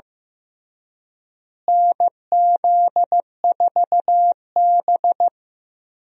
Morse code, heard as a keyed tone, about 16 wpm, 700 Hz. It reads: NZ4B
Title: USA Call Signs - 4 Characters - Repeated 3x - 15wpm